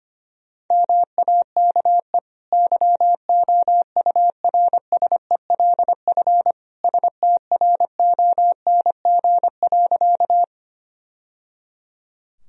Morse code, heard as a keyed tone, about 25 wpm, 700 Hz. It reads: MAKE YOURSELF STRONG.
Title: Sentences from Top 1000 Words - Review Part 3 - 25wpm